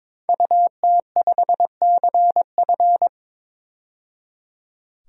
Morse code, heard as a keyed tone, about 22 wpm, 700 Hz. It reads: UT5CF
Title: DXCC-100 Call Signs - 5 Characters - 22wpm